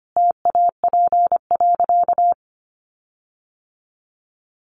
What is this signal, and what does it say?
Morse code, heard as a keyed tone, about 25 wpm, 700 Hz. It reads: TAP.
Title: Sentences from Top 1400 Words 25wpm